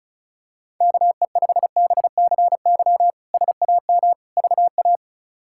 Morse code, heard as a keyed tone, about 34 wpm, 700 Hz. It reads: KE5BCY SAM VA
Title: K1USN SST QSO Practice - 35wpm